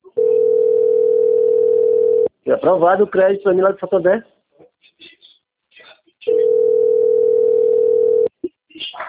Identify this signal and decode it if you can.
Ringback tone (North America), 440 + 480 Hz, with speech.
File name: out- - - - - .8523